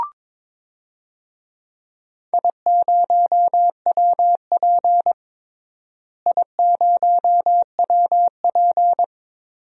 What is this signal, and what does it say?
Morse code, heard as a keyed tone, about 22 wpm, 700 Hz. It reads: I0WP I0WP
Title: DXCC-100 Call Signs - 4 Characters - Repeated 3x - 22wpm